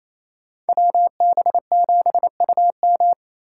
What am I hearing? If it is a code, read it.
Morse code, heard as a keyed tone, about 32 wpm, 700 Hz. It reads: WB7UM
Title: Speed Racing - USA 2x2 Call Signs 28wpm